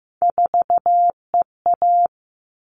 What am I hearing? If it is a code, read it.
Morse code, heard as a keyed tone, about 15 wpm, 700 Hz. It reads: 4EA
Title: All Cut Numbers - 3-Digit 15wpm